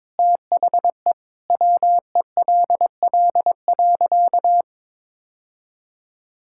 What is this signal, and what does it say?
Morse code, heard as a keyed tone, about 22 wpm, 700 Hz. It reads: THE WELL.
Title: Sentences from Top 1600 Words 22wpm